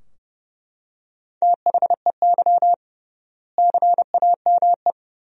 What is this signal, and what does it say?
Morse code, heard as a keyed tone, about 31 wpm, 700 Hz. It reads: THEY CAME
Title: 3x Word-Spacing - Sentences from Top 300 Words - 30wpm